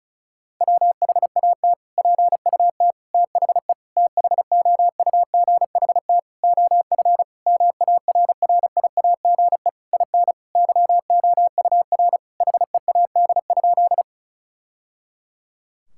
Morse code, heard as a keyed tone, about 36 wpm, 700 Hz. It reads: WHAT PUT THE THOUGHT OF MARRIAGE IN YOUR HEAD?
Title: Sentences from Top 1000 Words - Review Part 5 - 35wpm